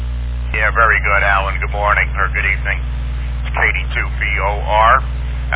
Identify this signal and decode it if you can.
Emergency Alert System (SAME) tones, heard as harsh buzzing bursts, with speech. It is heard from 0.9 to 4.4 s.